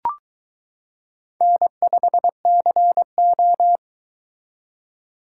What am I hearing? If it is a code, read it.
Morse code, heard as a keyed tone, about 23 wpm, 700 Hz. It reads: N5CO